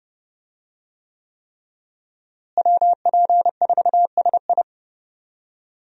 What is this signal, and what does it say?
Morse code, heard as a keyed tone, about 30 wpm, 700 Hz. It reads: WP4SI
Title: USA Call Signs - 5 Characters - Repeated 3x - 30wpm